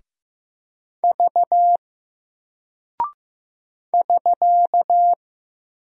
Morse code, heard as a keyed tone, about 15 wpm, 700 Hz. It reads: V <SK>
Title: Single Character (Letter:Number:Punctuation:Prosign) Alt 15wpm